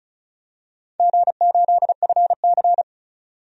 Morse code, heard as a keyed tone, about 34 wpm, 700 Hz. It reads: G8FC
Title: DXCC-100 Call Signs - 4 Characters - 35wpm